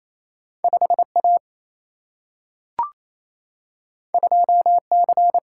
Morse code, heard as a keyed tone, about 28 wpm, 700 Hz. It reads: 5 A 2 C
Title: Single-Letter Single-Number 28wpm